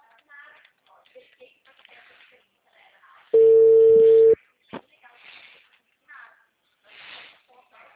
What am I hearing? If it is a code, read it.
Ringback tone (China, 450 Hz), with speech.
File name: out- -2021- - - .46